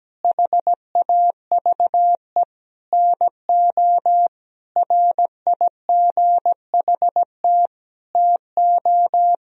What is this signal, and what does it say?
Morse code, heard as a keyed tone, about 17 wpm, 700 Hz. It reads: HAVE NO RIGHT TO
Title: Sentences from Top 1000 Words - Review Part 3 - 17wpm